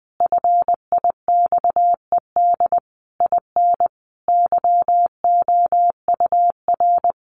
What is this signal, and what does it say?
Morse code, heard as a keyed tone, about 20 wpm, 700 Hz. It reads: FIXED IN YOUR
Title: Sentences from Top 1100 Words 20wpm